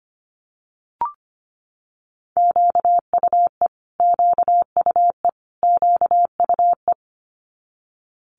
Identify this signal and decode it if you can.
Morse code, heard as a keyed tone, about 25 wpm, 700 Hz. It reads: QUE QUE QUE
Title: Multi-letter Phonemes Repeated 3x 25wpm